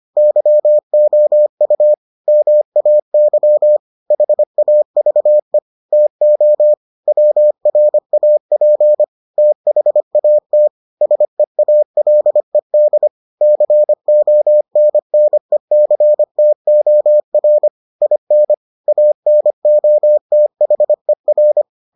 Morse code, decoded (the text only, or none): YOU MAY HAVE TO WRAP THAT SEALED CONNECTOR IN ANOTHER